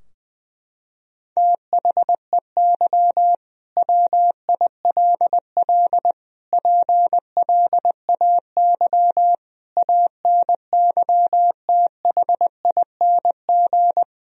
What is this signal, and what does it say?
Morse code, heard as a keyed tone, about 20 wpm, 700 Hz. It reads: THEY WILL PLAY ANYTHING
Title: Sentences from Top 1000 Words - Review Part 4 - 20wpm